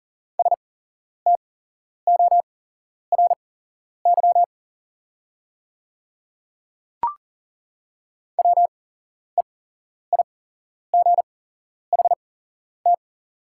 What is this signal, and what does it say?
Morse code, heard as a keyed tone, about 41 wpm, spaced as for 11 wpm, 690 Hz. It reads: STORY WEIGHT
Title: 8x Character Spacing - Top 500 Words - 40wpm